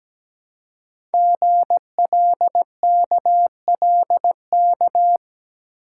Morse code, heard as a keyed tone, about 17 wpm, 700 Hz. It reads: GLKLK
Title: Code Groups - 5 Letters - 17wpm